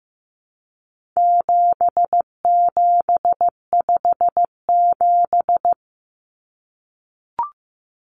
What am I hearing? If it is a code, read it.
Morse code, heard as a keyed tone, about 15 wpm, 700 Hz. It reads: 7757
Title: Numbers - 4-Digit 15wpm